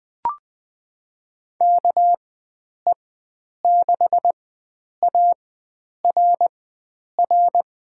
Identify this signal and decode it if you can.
Morse code, heard as a keyed tone, about 21 wpm, spaced as for 9 wpm, 700 Hz. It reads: KE6ARR
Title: USA Call Signs - 6 Characters - 4x Character Spacing - 20wpm